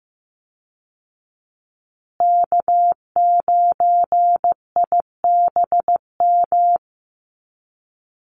Morse code, heard as a keyed tone, about 15 wpm, 700 Hz. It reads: K9IBM
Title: Call Signs Repeated 2x - 15wpm